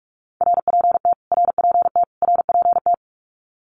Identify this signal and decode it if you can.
Morse code, heard as a keyed tone, about 44 wpm, 700 Hz. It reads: RPT RPT RPT
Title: Abbreviations Repeated 3x - 45wpm